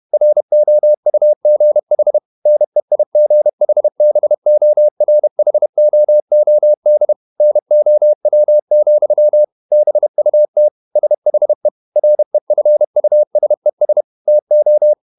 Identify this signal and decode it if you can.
Morse code, heard as a keyed tone, about 31 wpm, 600 Hz. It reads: ROUGH NEIGHBORHOOD NOW, BUT SHE REFUSES TO
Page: Never 31wpm | CW med Gnister